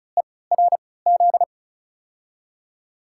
Morse code, heard as a keyed tone, about 34 wpm, spaced as for 20 wpm, 700 Hz. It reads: ERZ
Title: USA Call Signs - 5 Characters - 3x Character Spacing - 35wpm